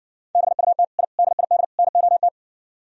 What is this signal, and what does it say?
Morse code, heard as a keyed tone, about 59 wpm, 700 Hz. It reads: BUT I DID NOT